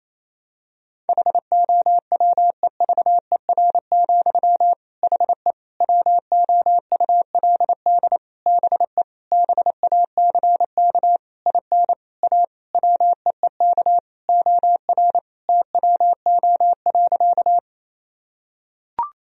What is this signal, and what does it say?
Morse code, heard as a keyed tone, about 28 wpm, 700 Hz. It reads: HOWEVER, HE WOULD BE BACK IN A WEEK OR TWO.
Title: Sentences from Top 1000 Words - Review Part 2 - 28wpm